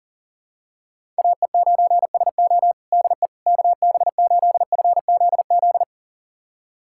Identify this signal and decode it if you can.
Morse code, heard as a keyed tone, about 37 wpm, 700 Hz. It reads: AE9SO DE KB8FZZ
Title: USA Callsign DE Callsign 40wpm